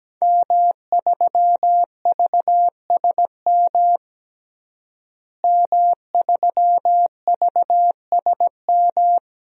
Morse code, heard as a keyed tone, about 17 wpm, 700 Hz. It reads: M3VSM M3VSM
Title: DXCC-100 Call Signs - 5 Characters - Repeated 3x - 17wpm